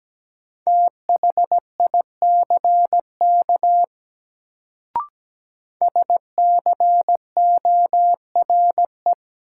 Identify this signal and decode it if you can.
Morse code, heard as a keyed tone, about 17 wpm, 700 Hz. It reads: THICK SCORE
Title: Sentences from Top 1600 Words - New Word Drill - 17wpm